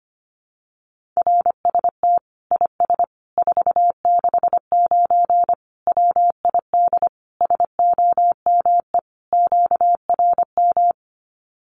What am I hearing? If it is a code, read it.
Morse code, heard as a keyed tone, about 25 wpm, 700 Hz. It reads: RST IS 469 WID SOME QRM
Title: Common QSO Phrases 25wpm